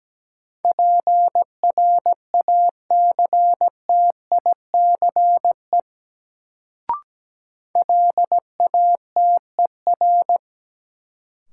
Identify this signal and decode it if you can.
Morse code, heard as a keyed tone, about 17 wpm, 700 Hz. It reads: PRACTICE LATER
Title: Sentences from Top 700 Words - New Word Drill - 17wpm